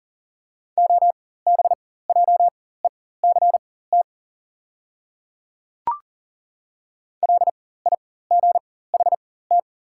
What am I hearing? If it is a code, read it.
Morse code, heard as a keyed tone, about 39 wpm, spaced as for 18 wpm, 690 Hz. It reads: OBJECT LIGHT
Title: 4x Character Spacing - Top 500 Words - 40wpm